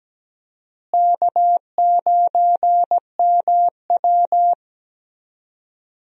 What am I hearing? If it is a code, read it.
Morse code, heard as a keyed tone, about 17 wpm, 700 Hz. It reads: K9MW
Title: USA Call Signs - 4 Characters - Repeated 2x - 17wpm